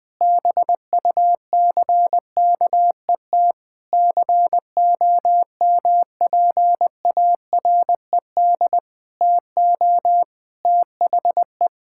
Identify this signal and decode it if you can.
Morse code, heard as a keyed tone, about 20 wpm, 700 Hz. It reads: BUCKET COMPARED TO THE
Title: Sentences with 8-Word Phrases - 20wpm